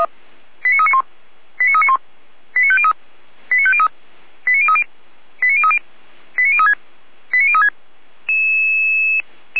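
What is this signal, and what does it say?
Five-tone selective call (ZVEI), heard as quick runs of five tones, tones of about 70 ms. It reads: 89391 89391 89503 89503 89030 89030 89036 89036
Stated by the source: ZVEI five-tone call 89042 received via SDR.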